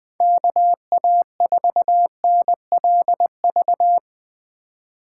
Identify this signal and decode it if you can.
Morse code, heard as a keyed tone, about 20 wpm, 700 Hz. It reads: KA4NLV
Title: Speed Racing - USA 2x3 Call Signs 17wpm